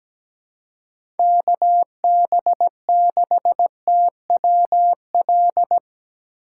Morse code, heard as a keyed tone, about 17 wpm, 700 Hz. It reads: KB6TWL
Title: USA Call Signs - 2x Character Spacing - 17wpm